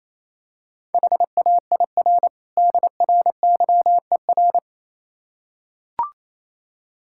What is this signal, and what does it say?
Morse code, heard as a keyed tone, about 27 wpm, 700 Hz. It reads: HAIR DRYER
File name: Easy Words - Speed-Racing with Hints - 22wpm